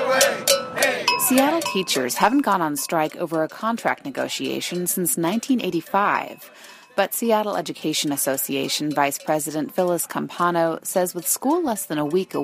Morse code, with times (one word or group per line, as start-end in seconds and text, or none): none